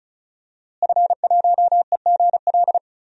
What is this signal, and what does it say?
Morse code, heard as a keyed tone, about 34 wpm, 700 Hz. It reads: F1EGL
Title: DXCC-100 Call Signs - 5 Characters - Repeated 2x - 35wpm